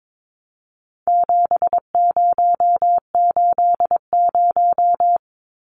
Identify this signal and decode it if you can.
Morse code, heard as a keyed tone, about 22 wpm, 700 Hz. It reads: 7080
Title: Numbers - 4-Digit 22wpm